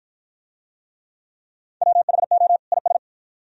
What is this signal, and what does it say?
Morse code, heard as a keyed tone, about 49 wpm, 700 Hz. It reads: WHO IS